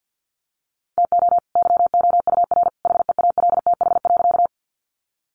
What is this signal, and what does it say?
Morse code, heard as a keyed tone, about 48 wpm, 700 Hz. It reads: TO YOUR HEALTH.
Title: Sentences from Top 600 Words 50wpm